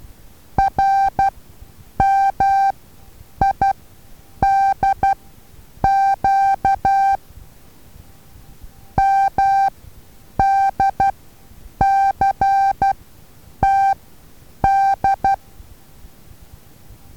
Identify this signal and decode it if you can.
Morse code, heard as a keyed tone, about 12 wpm, spaced as for 8 wpm, 780 Hz. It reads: RMIDQ MDCTD
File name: Taviro HA6VA A